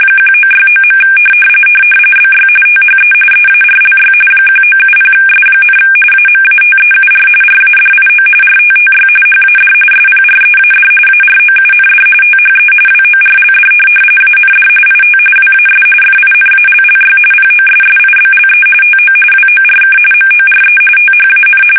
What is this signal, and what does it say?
RTTY, about 45.45 baud. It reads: 1969. BY THE EARLY 1970S, AMATEUR RADIO RTTY HAD SPREAD AROUND THE WORLD AND IT WAS FINALLY POSSIBLE TO WORK MORE THAN
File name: fsk850 / teletype-RTTY